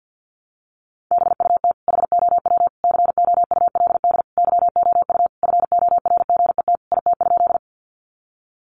Morse code, heard as a keyed tone, about 50 wpm, 700 Hz. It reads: BUT HOW COULD YOU FORGET IT?
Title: Sentences from Top 1200 Words 50wpm